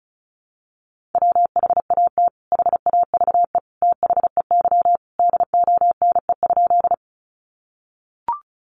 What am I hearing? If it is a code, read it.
Morse code, heard as a keyed tone, about 34 wpm, 700 Hz. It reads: WHAT HAVE THEY DONE?
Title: Sentences from Top 500 Words 35wpm